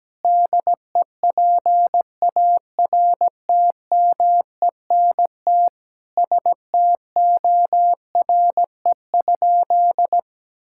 Morse code, heard as a keyed tone, about 17 wpm, 700 Hz. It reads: DEPARTMENT STORE?
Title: Sentences from Top 1000 Words - Review Part 5 - 17wpm